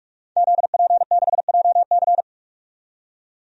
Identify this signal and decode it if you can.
Morse code, heard as a keyed tone, about 42 wpm, 700 Hz. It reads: ZP6JC
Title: DXCC-100 Call Signs - 5 Characters - 45wpm